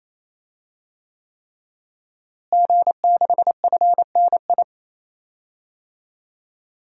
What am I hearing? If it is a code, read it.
Morse code, heard as a keyed tone, about 28 wpm, 700 Hz. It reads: G6FNI
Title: DXCC-100 Call Signs - 5 Characters - Repeated 3x - 28wpm